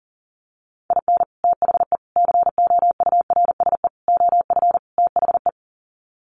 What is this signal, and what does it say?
Morse code, heard as a keyed tone, about 37 wpm, 700 Hz. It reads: IN THE COURSE OF THE
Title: Sets of 5 Words 40wpm